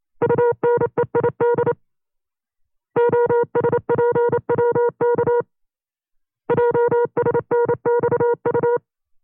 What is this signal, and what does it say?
Morse code, heard as a keyed tone, about 28 wpm, 480 Hz. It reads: UNEID OSPWK JSNXU
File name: FreeFiles / CW-practice / 25WPM-side2